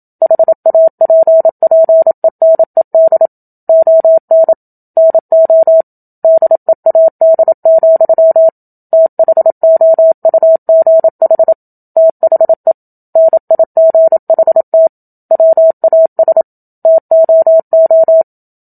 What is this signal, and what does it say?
Morse code, heard as a keyed tone, about 27 wpm, 650 Hz. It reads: HAPPENED ON NO DEAD, THOUGH THE NIGHT WAS TOO
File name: Book Two / 01-Chapter 1 - 27 WPM